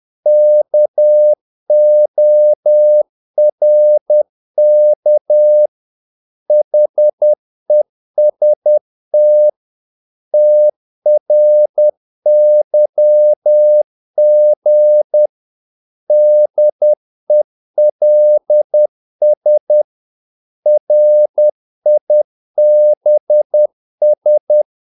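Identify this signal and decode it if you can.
Morse code, heard as a keyed tone, about 10 wpm, 600 Hz. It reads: KORK HEST TRYG DELS RIBS
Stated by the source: Morse practice of short Danish words at 10 wpm, from CW med Gnister.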